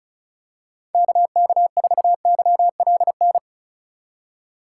Morse code, heard as a keyed tone, about 34 wpm, 700 Hz. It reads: KK4YLN
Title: USA Call Signs - 6 Characters - 4x Character Spacing - 35wpm